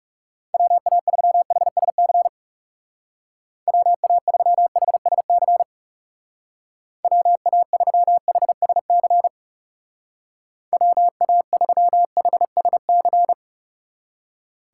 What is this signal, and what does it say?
Morse code, heard as a keyed tone, about 37 wpm, 700 Hz. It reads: WA3HSC WA3HSC WA3HSC WA3HSC
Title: Speed Racing - USA 2x3 Call Signs 30wpm